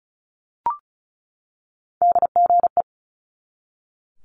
Morse code, heard as a keyed tone, about 36 wpm, 700 Hz. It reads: DGE
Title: Multi-letter Phonemes 35wpm